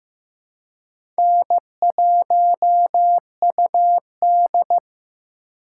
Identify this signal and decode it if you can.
Morse code, heard as a keyed tone, about 15 wpm, 700 Hz. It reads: N1UD
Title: USA Call Signs - 4 Characters - 3x Character Spacing - 15wpm